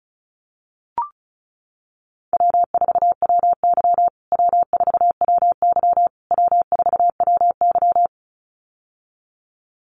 Morse code, heard as a keyed tone, about 34 wpm, 700 Hz. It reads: W4WY W4WY W4WY
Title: USA Call Signs Repeated 3x 35wpm